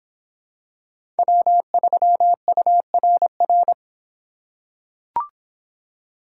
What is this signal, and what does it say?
Morse code, heard as a keyed tone, about 25 wpm, 700 Hz. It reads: W3URR